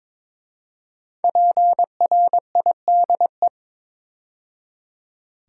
Morse code, heard as a keyed tone, about 23 wpm, 700 Hz. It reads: PRIDE
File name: 5 Letter Words - First 3 Letters Given - 22wpm